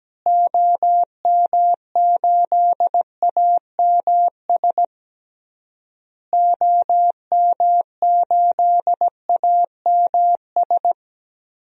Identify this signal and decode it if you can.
Morse code, heard as a keyed tone, about 17 wpm, 700 Hz. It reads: OM8AMS OM8AMS
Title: DXCC-100 Call Signs - 6 Characters - Repeated 3x - 17wpm